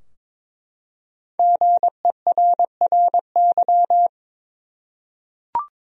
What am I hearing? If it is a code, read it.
Morse code, heard as a keyed tone, about 22 wpm, 700 Hz. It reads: GERRY
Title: USA Common Names - Max 5 Letters - 22wpm